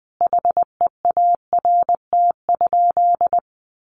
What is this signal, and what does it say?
Morse code, heard as a keyed tone, about 20 wpm, 700 Hz. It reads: HEART?
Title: Sentences from Top 1400 Words 20wpm